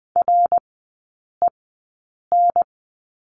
Morse code, heard as a keyed tone, about 20 wpm, spaced as for 8 wpm, 700 Hz. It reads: REN
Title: 053 Lesson 10e - Letter r - Three Characters 20wpm